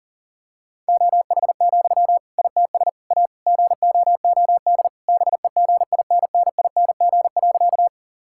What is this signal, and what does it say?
Morse code, heard as a keyed tone, about 40 wpm, 700 Hz. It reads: OH, ITS A GOOD BEGINNING.
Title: Sentences from Top 1000 Words - Review Part 5 - 40wpm